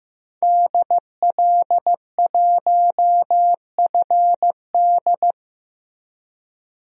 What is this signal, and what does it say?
Morse code, heard as a keyed tone, about 15 wpm, 700 Hz. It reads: DL1FD
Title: DXCC-100 Call Signs - 5 Characters - 15wpm